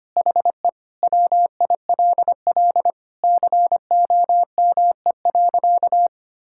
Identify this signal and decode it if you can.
Morse code, heard as a keyed tone, about 25 wpm, 700 Hz. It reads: HE WILL COME.
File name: practice / 25wpm / Sentences with 3-Word Phrases - 25wpm